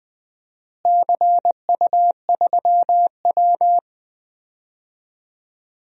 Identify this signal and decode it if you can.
Morse code, heard as a keyed tone, about 21 wpm, 700 Hz. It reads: CU3W
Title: DXCC-100 Call Signs - 4 Characters - Repeated 3x - 20wpm